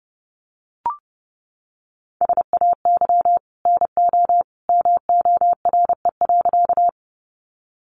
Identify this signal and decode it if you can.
Morse code, heard as a keyed tone, about 30 wpm, 700 Hz. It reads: SAY NO MORE.
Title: Sentences from Top 200 Words 30wpm